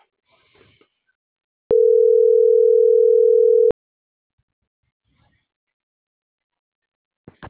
Ringback tone (North America), 440 + 480 Hz.